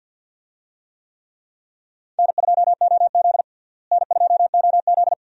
Morse code, heard as a keyed tone, about 52 wpm, 700 Hz. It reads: N2OZ N2OZ
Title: USA Call Signs - 4 Characters - Repeated 2x - 50wpm